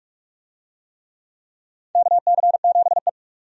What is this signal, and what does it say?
Morse code, heard as a keyed tone, about 46 wpm, 700 Hz. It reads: KC7E
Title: USA Call Signs - 4 Characters - 45wpm